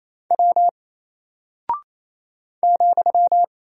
Morse code, heard as a keyed tone, about 27 wpm, 700 Hz. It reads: W ,
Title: Single Character (Letter:Number:Punctuation:Prosign) Alt 28wpm